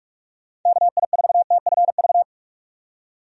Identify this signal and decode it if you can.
Morse code, heard as a keyed tone, about 41 wpm, 700 Hz. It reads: KI4TFV
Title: USA Call Signs - 6 Characters - 45wpm